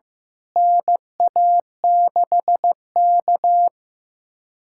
Morse code, heard as a keyed tone, about 15 wpm, 700 Hz. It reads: NA6K
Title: USA Callsign DE Callsign - 4 Characters - 15wpm